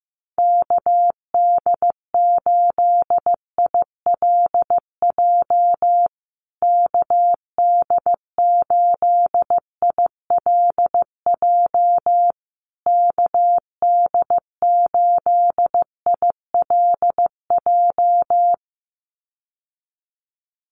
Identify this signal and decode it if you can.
Morse code, heard as a keyed tone, about 15 wpm, 700 Hz. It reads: KD8ILJ KD8ILJ KD8ILJ
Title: USA Call Signs Repeated 3x 15wpm